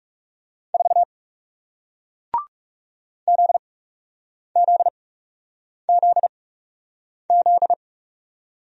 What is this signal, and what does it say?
Morse code, heard as a keyed tone, about 40 wpm, spaced as for 8 wpm, 700 Hz. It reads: 4 ZZZZ